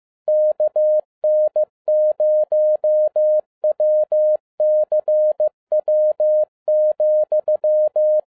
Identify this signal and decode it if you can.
Morse code, heard as a keyed tone, about 15 wpm, 610 Hz. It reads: KN0WCW,
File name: 2011-10-mp3 / n12-JL3YMV 15wpm